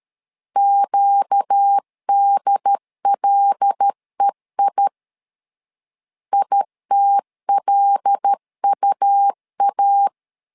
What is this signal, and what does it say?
Morse code, heard as a keyed tone, about 13 wpm, 800 Hz. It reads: QDLEI ITLUA
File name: LEZIONI / 5 LBVC / 5 L 12wpm 1913